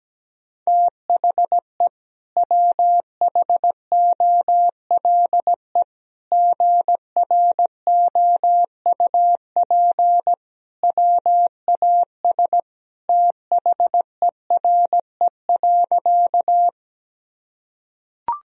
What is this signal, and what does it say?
Morse code, heard as a keyed tone, about 17 wpm, 700 Hz. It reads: THE WHOLE GROUP WAS THERE.
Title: Sentences from Top 1000 Words - Review Part 1 - 17wpm